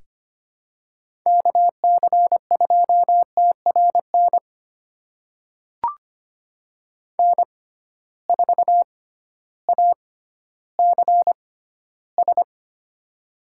Morse code, heard as a keyed tone, about 25 wpm, spaced as for 13 wpm, 700 Hz. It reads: KC2TRN N4ACS
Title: USA Call Signs - 6x Character Spacing - 25wpm